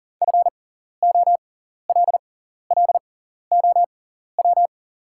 Morse code, heard as a keyed tone, about 40 wpm, spaced as for 14 wpm, 700 Hz. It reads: FOLLOW
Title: 6x Character Spacing - Top 500 Words - 40wpm